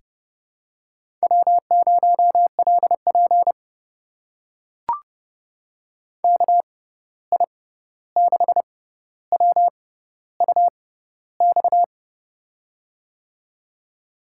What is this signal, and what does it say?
Morse code, heard as a keyed tone, about 30 wpm, spaced as for 14 wpm, 700 Hz. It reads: W0LP KI6WUX
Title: USA Call Signs - 6x Character Spacing - 30wpm